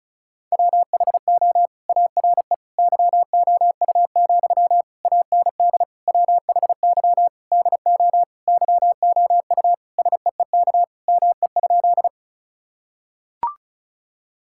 Morse code, heard as a keyed tone, about 36 wpm, 700 Hz. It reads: WHO ARE YOU, AND WHY DO YOU SEEK ME?
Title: Sentences from Top 1600 Words 35wpm